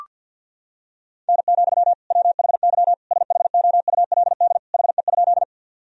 Morse code, heard as a keyed tone, about 46 wpm, 700 Hz. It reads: N, WHY SHOULD HE?
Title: Sentences from Top 300 Words and Abbreviations 50wpm